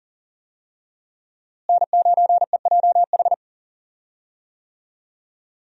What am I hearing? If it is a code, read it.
Morse code, heard as a keyed tone, about 40 wpm, 700 Hz. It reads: N9EJH
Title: USA Call Signs - 5 Characters - Repeated 3x - 40wpm